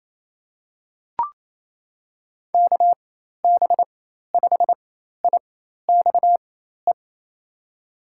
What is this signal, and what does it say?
Morse code, heard as a keyed tone, about 28 wpm, spaced as for 13 wpm, 700 Hz. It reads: KB5IXE
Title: USA Call Signs - 6 Characters - 4x Character Spacing - 28wpm